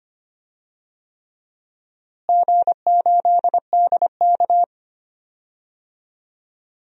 Morse code, heard as a keyed tone, about 25 wpm, 700 Hz. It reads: G8DK
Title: DXCC-100 Call Signs - 4 Characters - Repeated 3x - 25wpm